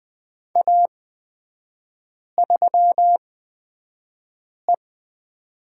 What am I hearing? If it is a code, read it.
Morse code, heard as a keyed tone, about 20 wpm, spaced as for 5 wpm, 700 Hz. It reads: A3E
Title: 236 Lesson 38e - Letter q - Three Characters 20wpm